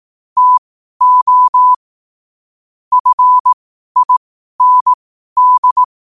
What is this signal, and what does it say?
Morse code, heard as a keyed tone, about 18 wpm, spaced as for 12 wpm, 1000 Hz. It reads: TO FIND
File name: qotdfiles / 12wpm